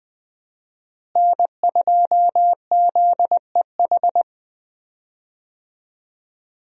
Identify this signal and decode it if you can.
Morse code, heard as a keyed tone, about 20 wpm, 700 Hz. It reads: N2ZEH
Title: USA Call Signs - 5 Characters - Repeated 2x - 20wpm